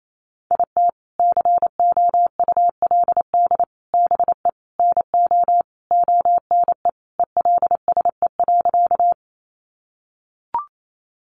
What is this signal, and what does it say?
Morse code, heard as a keyed tone, about 28 wpm, 700 Hz. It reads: IT COULD BE NO ONE ELSE.
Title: Sentences from Top 800 Words 28wpm